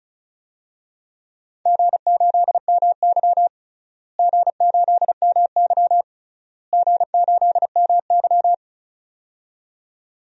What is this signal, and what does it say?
Morse code, heard as a keyed tone, about 34 wpm, 700 Hz. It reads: G8MY G8MY G8MY
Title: DXCC-100 Call Signs - 4 Characters - Repeated 3x - 35wpm